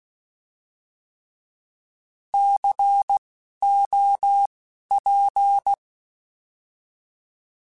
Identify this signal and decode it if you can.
Morse code, heard as a keyed tone, about 16 wpm, spaced as for 12 wpm, 800 Hz. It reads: COP
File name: Alphabetical / Alphabetical mp3s / Cs @10